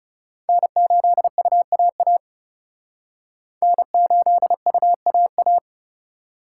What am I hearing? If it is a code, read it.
Morse code, heard as a keyed tone, about 36 wpm, 700 Hz. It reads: N8UAA N8UAA
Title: Speed Racing - USA 1x3 Call Signs 30wpm